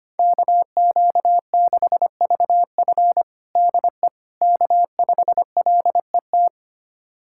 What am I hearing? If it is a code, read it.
Morse code, heard as a keyed tone, about 25 wpm, 700 Hz. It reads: KQ6VF DE K5LET
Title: USA Callsign DE Callsign - 5 Characters - 25wpm